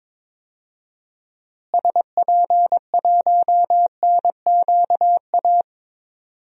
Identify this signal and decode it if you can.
Morse code, heard as a keyed tone, about 22 wpm, 700 Hz. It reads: SP1NQA
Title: DXCC-100 Call Signs - 6 Characters - Repeated 3x - 22wpm